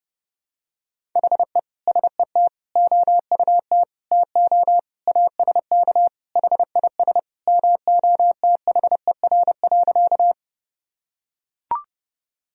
Morse code, heard as a keyed tone, about 32 wpm, 700 Hz. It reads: HE SET OUT TO ASK HIS MOTHER.
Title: Sentences from Top 1000 Words - Review Part 1 - 30wpm